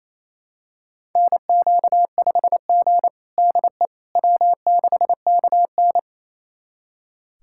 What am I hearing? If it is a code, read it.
Morse code, heard as a keyed tone, about 27 wpm, 700 Hz. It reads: NQ5G DE W6KN
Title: USA Callsign DE Callsign - 4 Characters - 28wpm